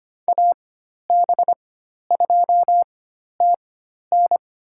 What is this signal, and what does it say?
Morse code, heard as a keyed tone, about 25 wpm, spaced as for 12 wpm, 700 Hz. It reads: AB2TN
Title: USA Call Signs - 5 Characters - 4x Character Spacing - 25wpm